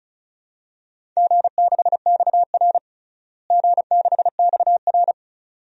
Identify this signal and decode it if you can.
Morse code, heard as a keyed tone, about 34 wpm, 700 Hz. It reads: G6XR G6XR
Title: DXCC-100 Call Signs - 4 Characters - Repeated 2x - 35wpm